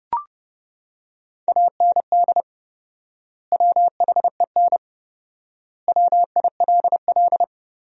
Morse code, heard as a keyed tone, about 30 wpm, 700 Hz. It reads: AND WHEN WILL
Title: 4x Word-Spacing - Sentences from Top 100 Words - 30wpm